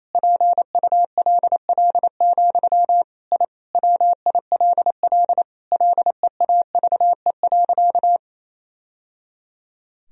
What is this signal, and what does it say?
Morse code, heard as a keyed tone, about 28 wpm, 700 Hz. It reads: PULL, I WILL LEAVE.
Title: Sentences from Top 1000 Words - Review Part 2 - 28wpm